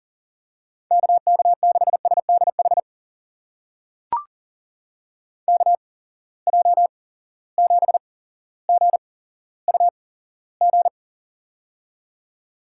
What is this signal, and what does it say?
Morse code, heard as a keyed tone, about 42 wpm, spaced as for 17 wpm, 700 Hz. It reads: KK6SDH KJ7GUG
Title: USA Call Signs - 6 Characters - 8x Character Spacing - 40wpm